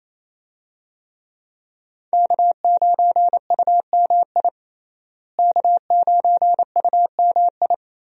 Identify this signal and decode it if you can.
Morse code, heard as a keyed tone, about 28 wpm, 700 Hz. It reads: K9UMI K9UMI
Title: USA Call Signs - 5 Characters - Repeated 3x - 28wpm